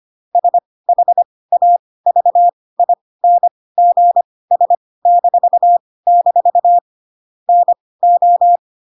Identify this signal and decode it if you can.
Morse code, heard as a keyed tone, about 25 wpm, spaced as for 18 wpm, 700 Hz. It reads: SHAVINGS-- NO
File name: Pinocchio / Pinocchio ch1 12 25 ik1hge